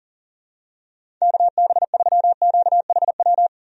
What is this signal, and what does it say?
Morse code, heard as a keyed tone, about 40 wpm, 700 Hz. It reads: KB3QHW
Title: USA Call Signs - 6 Characters - Repeated 2x - 40wpm